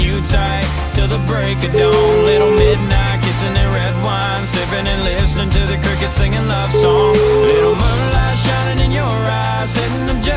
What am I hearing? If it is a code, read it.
Ringback tone (Europe, 425 Hz), with noise.